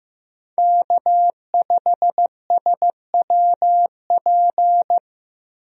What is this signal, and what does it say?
Morse code, heard as a keyed tone, about 15 wpm, 700 Hz. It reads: K5SWP
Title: USA Call Signs - 5 Characters - Repeated 2x - 15wpm